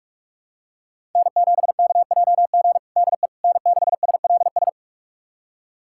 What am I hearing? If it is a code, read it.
Morse code, heard as a keyed tone, about 43 wpm, 700 Hz. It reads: N7KJG DE N6SLS
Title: USA Callsign DE Callsign - 5 Characters - 45wpm